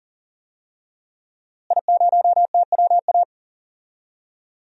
Morse code, heard as a keyed tone, about 40 wpm, 700 Hz. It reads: I0TWA
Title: DXCC-100 Call Signs - 5 Characters - Repeated 3x - 40wpm